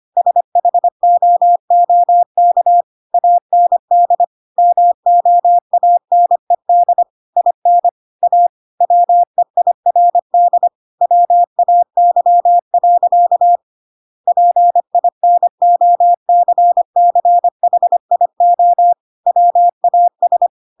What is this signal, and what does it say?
Morse code, heard as a keyed tone, about 25 wpm, 700 Hz. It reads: SHOOK AND MOANED IN A WEIRD WAY. PINOCCHIO WAS
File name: Pinocchio / Pinocchio ch6 25 25 ik1hge